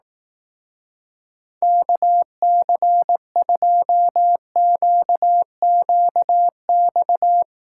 Morse code, heard as a keyed tone, about 18 wpm, 700 Hz. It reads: KC2QQX
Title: Speed Racing - USA 2x3 Call Signs 15wpm